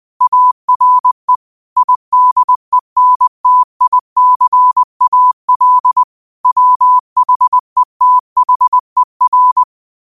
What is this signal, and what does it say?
Morse code, heard as a keyed tone, about 20 wpm, 1000 Hz. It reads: ARE IDENTICAL WHETHER